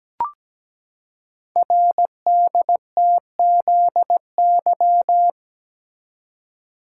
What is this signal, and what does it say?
Morse code, heard as a keyed tone, about 17 wpm, 700 Hz. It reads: RDTZY
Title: Code Groups - 5 Letters - 17wpm